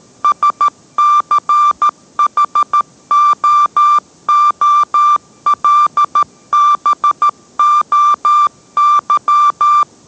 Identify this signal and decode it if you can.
Morse code, heard as a keyed tone, about 13 wpm, 1180 Hz. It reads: SCHOOLBOY